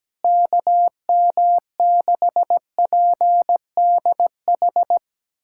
Morse code, heard as a keyed tone, about 17 wpm, 700 Hz. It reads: KM6PDH
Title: USA Callsign DE Callsign - 6 Characters - 17wpm